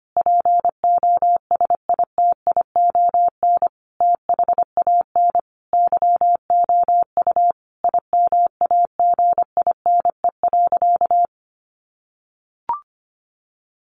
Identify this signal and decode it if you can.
Morse code, heard as a keyed tone, about 25 wpm, 700 Hz. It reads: POSITION THAN YOU IMAGINE.
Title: Sentences from Top 1400 Words 25wpm